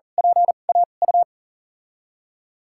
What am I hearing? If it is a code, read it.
Morse code, heard as a keyed tone, about 40 wpm, spaced as for 29 wpm, 700 Hz. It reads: PAU
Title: USA Call Signs - 6 Characters - 2x Character Spacing - 40wpm